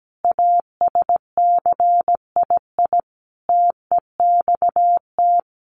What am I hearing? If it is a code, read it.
Morse code, heard as a keyed tone, about 17 wpm, 700 Hz. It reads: ASCII TEXT
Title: Sets of 5 Words 17wpm